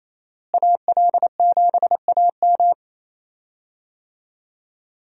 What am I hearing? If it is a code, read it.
Morse code, heard as a keyed tone, about 28 wpm, 700 Hz. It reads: AL7AM
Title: USA Call Signs - 5 Characters - Repeated 2x - 28wpm